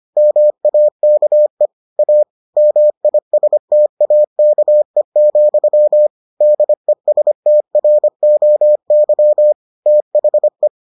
Morse code, heard as a keyed tone, about 25 wpm, 600 Hz. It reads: MAKE A MISTAKE, DESTROY THE